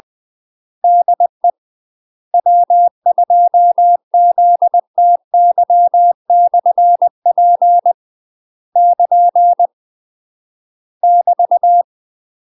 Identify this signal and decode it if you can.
Morse code, heard as a keyed tone, about 20 wpm, 700 Hz. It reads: DE W2ZTY/P <KN> <BT>
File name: UK MASTER QSOs TRAINING / 20WPM / UK MASTER QSOs 0004